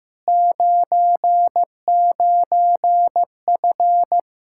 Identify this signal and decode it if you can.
Morse code, heard as a keyed tone, about 15 wpm, 700 Hz. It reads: 99F
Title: Practice QSO - Exchange 2 - Info - 15wpm